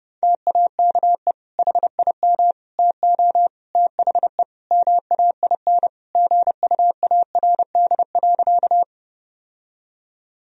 Morse code, heard as a keyed tone, about 30 wpm, 700 Hz. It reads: TAKE HIM TO THE MAIN GUARD.
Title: Sentences from Top 1000 Words - Review Part 4 - 30wpm